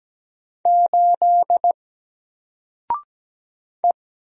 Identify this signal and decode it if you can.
Morse code, heard as a keyed tone, about 17 wpm, 700 Hz. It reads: 8 E
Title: Single Letter-Number 17wpm